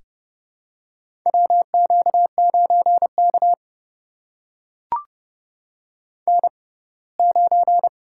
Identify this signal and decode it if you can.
Morse code, heard as a keyed tone, about 30 wpm, spaced as for 20 wpm, 690 Hz. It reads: WQ9K N9
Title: USA Call Signs - 4 Characters - 6x Character Spacing - 30wpm